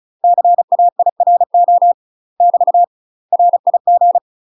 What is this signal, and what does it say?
Morse code, heard as a keyed tone, about 34 wpm, 700 Hz. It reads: CAIRO <BT> RIG
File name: UK MASTER QSOs TRAINING / 35WPM / UK MASTER QSOs 0025